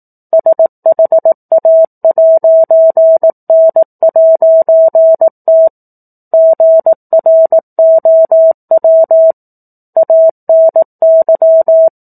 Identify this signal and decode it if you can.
Morse code, heard as a keyed tone, about 18 wpm, 650 Hz. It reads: SHA'N'T GROW ANY
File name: 04-Chapter-04 18 WPM